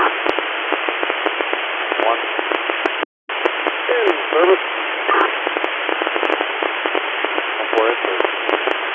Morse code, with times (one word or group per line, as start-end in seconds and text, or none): none